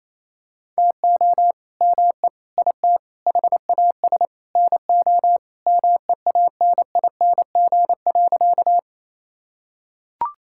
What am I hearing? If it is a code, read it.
Morse code, heard as a keyed tone, about 28 wpm, 700 Hz. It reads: TO ME IT HAS NO MEANING.
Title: Sentences from Top 1000 Words - Review Part 6 - 28wpm